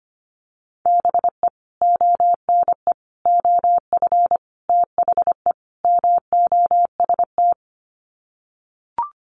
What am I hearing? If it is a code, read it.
Morse code, heard as a keyed tone, about 25 wpm, 690 Hz. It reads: BE ONE OF THE MOST
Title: Sets of 5 Words Encore 25wpm